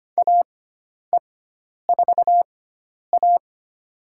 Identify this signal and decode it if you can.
Morse code, heard as a keyed tone, about 25 wpm, spaced as for 10 wpm, 700 Hz. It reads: AE4A
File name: Lesson 16g - Letter c - Call Signs 25wpm